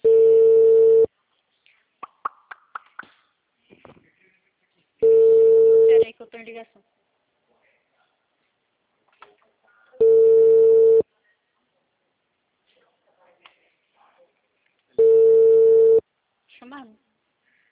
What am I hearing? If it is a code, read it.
Ringback tone (China, 450 Hz), with speech.
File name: out- -2001- - - .37059